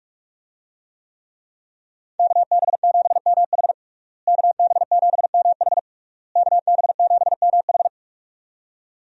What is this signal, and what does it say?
Morse code, heard as a keyed tone, about 46 wpm, 700 Hz. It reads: KB7MH KB7MH KB7MH
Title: USA Call Signs - 5 Characters - Repeated 3x - 45wpm